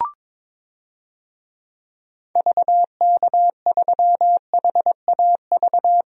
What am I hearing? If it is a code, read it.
Morse code, heard as a keyed tone, about 21 wpm, 700 Hz. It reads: VK3HAV
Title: DXCC-100 Call Signs - 6 Characters - Repeated 2x - 22wpm